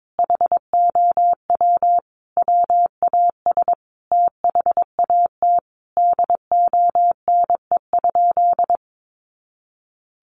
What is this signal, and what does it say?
Morse code, heard as a keyed tone, about 22 wpm, 700 Hz. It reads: HOW WAS THAT DONE?
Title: Sentences from Top 1100 Words 22wpm